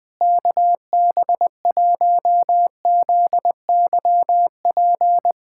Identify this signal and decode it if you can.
Morse code, heard as a keyed tone, about 20 wpm, 700 Hz. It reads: KB1ZYP
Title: USA Callsign DE Callsign 20wpm